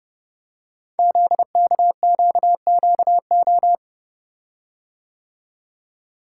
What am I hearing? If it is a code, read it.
Morse code, heard as a keyed tone, about 30 wpm, 700 Hz. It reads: ZKQQO
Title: Code Groups - 5 Letters - 30wpm